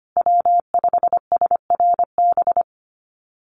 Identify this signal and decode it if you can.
Morse code, heard as a keyed tone, about 25 wpm, 700 Hz. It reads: W5SRB
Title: USA Call Signs Repeated 3x 25wpm